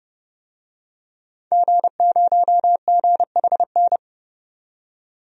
Morse code, heard as a keyed tone, about 30 wpm, 700 Hz. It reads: G0GHN
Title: DXCC-100 Call Signs - 5 Characters - 30wpm